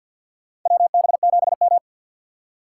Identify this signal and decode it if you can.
Morse code, heard as a keyed tone, about 52 wpm, 700 Hz. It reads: WB7M
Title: USA Call Signs - 4 Characters - 3x Character Spacing - 50wpm